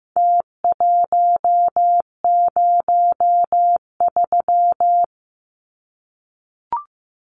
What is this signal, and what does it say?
Morse code, heard as a keyed tone, about 15 wpm, 700 Hz. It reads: T103
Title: Numbers - Contest Style - 4-Digit 15wpm